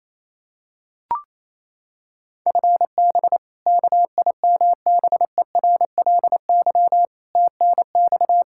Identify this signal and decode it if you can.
Morse code, heard as a keyed tone, about 28 wpm, 700 Hz. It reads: FB KIMBERLY TNX
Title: Practice QSO - Exchange 3 - More Info - 28wpm